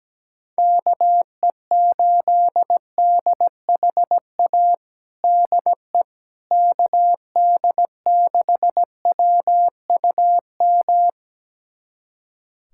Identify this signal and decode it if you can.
Morse code, heard as a keyed tone, about 17 wpm, 700 Hz. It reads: KE8DHA DE KD6WUM
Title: USA Callsign DE Callsign - 6 Characters - 17wpm